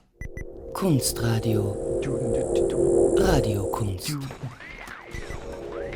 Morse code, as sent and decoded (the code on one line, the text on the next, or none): none